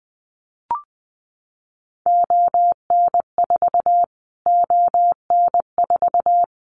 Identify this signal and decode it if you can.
Morse code, heard as a keyed tone, about 19 wpm, 700 Hz. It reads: ON4 ON4
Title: Callsign Prefixes for Top 25 DXCC - Repeated 3x - 20wpm